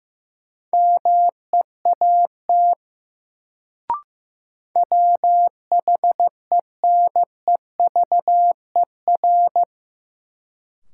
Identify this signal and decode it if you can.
Morse code, heard as a keyed tone, about 15 wpm, 700 Hz. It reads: MEAT WHENEVER
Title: Sentences from Top 1200 Words - New Word Drill Alt - 15wpm